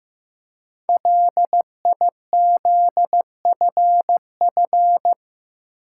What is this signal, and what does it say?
Morse code, heard as a keyed tone, about 15 wpm, 700 Hz. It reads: LIZFF
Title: Code Groups - 5 Letters - 15wpm